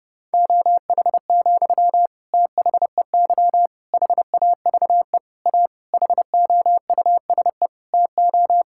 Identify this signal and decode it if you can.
Morse code, heard as a keyed tone, about 32 wpm, 700 Hz. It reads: OH, THEY HAVE A HOUSE TO
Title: Sentences from Top 1000 Words - Review Part 5 - 30wpm